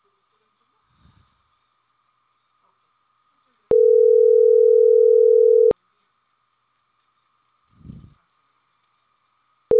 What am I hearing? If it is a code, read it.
Ringback tone (North America), 440 + 480 Hz.